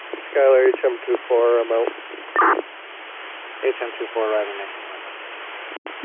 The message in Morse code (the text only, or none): none